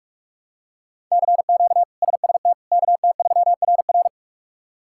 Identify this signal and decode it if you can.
Morse code, heard as a keyed tone, about 46 wpm, 700 Hz. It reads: CQ SST KT3RR
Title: K1USN SST QSO Practice - 45wpm